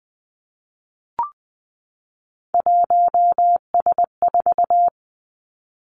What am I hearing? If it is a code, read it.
Morse code, heard as a keyed tone, about 20 wpm, 700 Hz. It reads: 1S4
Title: Three Challenging Letter-Number Combinations 20wpm